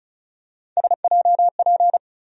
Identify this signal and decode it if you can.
Morse code, heard as a keyed tone, about 34 wpm, 700 Hz. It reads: SJP
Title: USA Call Sign Postfixes - 3 Letters 35wpm